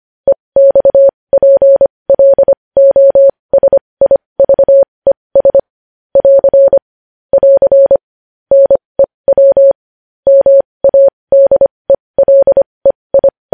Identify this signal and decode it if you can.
Morse code, heard as a keyed tone, about 25 wpm, spaced as for 20 wpm, 550 Hz. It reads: EXPLOSIVES <AR> <AR> NEW MADELEI
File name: podcast / News headlines-1x-25x20